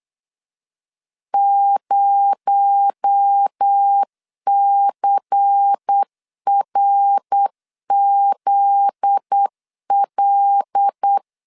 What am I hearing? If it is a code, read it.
Morse code, heard as a keyed tone, about 8 wpm, 800 Hz. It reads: 0CRZL
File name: LEZIONI / 8 YJ45 / 8 Y 8wpm 3245